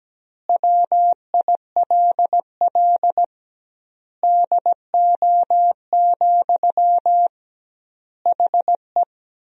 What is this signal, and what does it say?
Morse code, heard as a keyed tone, about 17 wpm, 700 Hz. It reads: WILL DO, HE
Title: 2x Word-Spacing - Sentences from Top 100 Words - 17wpm